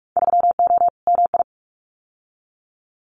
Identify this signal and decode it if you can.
Morse code, heard as a keyed tone, about 44 wpm, 700 Hz. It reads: 3O MI
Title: Winter Field Day Exchanges 45wpm